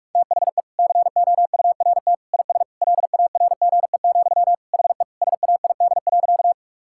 Morse code, heard as a keyed tone, about 44 wpm, 700 Hz. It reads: THE COURT IS LARGE, HE SAID.
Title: Sentences from Top 1600 Words 45wpm